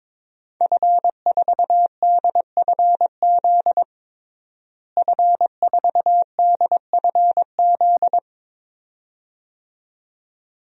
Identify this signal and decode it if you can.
Morse code, heard as a keyed tone, about 22 wpm, 700 Hz. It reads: F4DFZ F4DFZ
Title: DXCC-100 Call Signs - 5 Characters - Repeated 2x - 22wpm